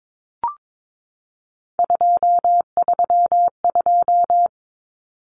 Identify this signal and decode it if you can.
Morse code, heard as a keyed tone, about 22 wpm, 700 Hz. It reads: 232
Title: Common Cut Numbers - 3-Digit 22wpm